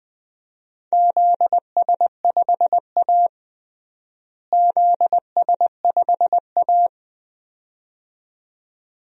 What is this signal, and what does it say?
Morse code, heard as a keyed tone, about 21 wpm, 700 Hz. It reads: ZS5A ZS5A
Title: DXCC-100 Call Signs - 4 Characters - Repeated 2x - 20wpm